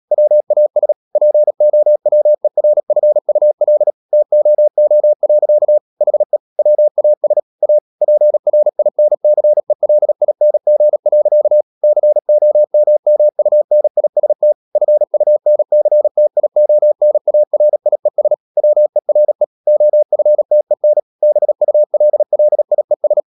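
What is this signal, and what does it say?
Morse code, heard as a keyed tone, about 37 wpm, 600 Hz. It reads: WAS POWERFUL TOO. HE WAS A PRINCELING. COMMUNIST FUNCTIONARIES WERE OFTEN BULLIES
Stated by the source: Morse code practice at 35–39 wpm sending English text of Ken Follett's novel Never, from CW med Gnister.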